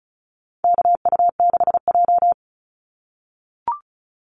Morse code, heard as a keyed tone, about 34 wpm, 700 Hz. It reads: KU6J
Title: NA SOTA Activator Callsigns 35wpm